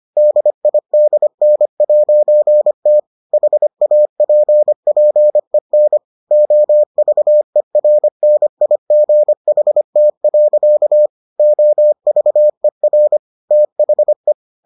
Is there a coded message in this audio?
Yes: Morse code from 0.2 to 14.3 s, heard as a keyed tone, about 25 wpm, 600 Hz. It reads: DIDN'T HAPPEN OVERNIGHT. OVER THE